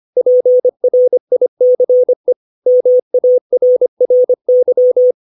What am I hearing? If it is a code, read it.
Morse code, heard as a keyed tone, about 25 wpm, 500 Hz. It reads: PRICE MARRY
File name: 632 words / 632 Words 25WPM